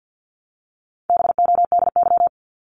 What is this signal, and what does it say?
Morse code, heard as a keyed tone, about 50 wpm, 700 Hz. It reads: BODY
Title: Top 500 Words Repeated 50wpm